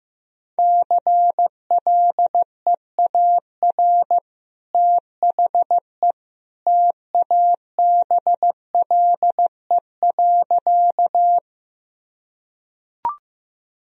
Morse code, heard as a keyed tone, about 16 wpm, 700 Hz. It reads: CLEAR THE TABLE.
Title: Sentences from Top 1000 Words - Review Part 2 - 15wpm